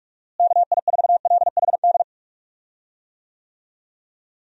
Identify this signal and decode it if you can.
Morse code, heard as a keyed tone, about 46 wpm, 700 Hz. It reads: KI4LHD
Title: USA Call Signs - 6 Characters - Repeated 2x - 45wpm